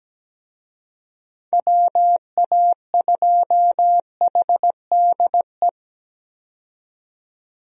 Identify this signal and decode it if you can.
Morse code, heard as a keyed tone, about 17 wpm, 700 Hz. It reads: WA2HDE
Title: USA Call Signs - 6 Characters - 17wpm